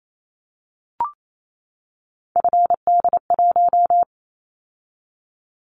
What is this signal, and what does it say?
Morse code, heard as a keyed tone, about 28 wpm, 700 Hz. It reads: FD1
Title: Callsign Prefixes for Top 25 DXCC - 28wpm